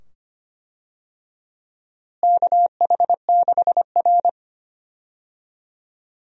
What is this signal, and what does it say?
Morse code, heard as a keyed tone, about 25 wpm, 700 Hz. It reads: KH6R
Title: DXCC-100 Call Signs - 4 Characters - Repeated 2x - 25wpm